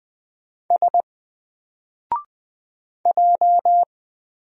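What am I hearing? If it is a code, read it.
Morse code, heard as a keyed tone, about 20 wpm, 700 Hz. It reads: S J
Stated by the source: Single letter and number Morse drill at 20 wpm, from N4LCD.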